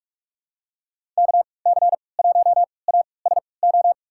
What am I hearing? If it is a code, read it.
Morse code, heard as a keyed tone, about 42 wpm, spaced as for 25 wpm, 700 Hz. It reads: KC1ASO
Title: USA Call Signs - 3x Character Spacing - 45wpm